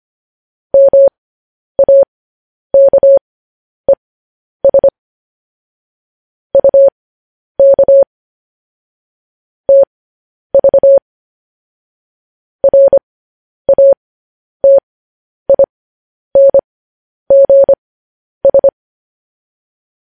Morse code, heard as a keyed tone, about 25 wpm, spaced as for 10 wpm, 550 Hz. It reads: MAKES UK TV RATINGS
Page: news-headlines-in-morse-code-at-10-wpm